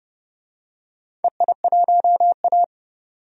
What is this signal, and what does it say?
Morse code, heard as a keyed tone, about 29 wpm, 700 Hz. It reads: EI1A
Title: DXCC-100 Call Signs - 4 Characters - 30wpm